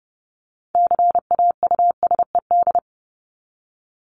Morse code, heard as a keyed tone, about 30 wpm, 700 Hz. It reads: CAUSED
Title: ED Suffix - Top 100 Words with Base Words 30wpm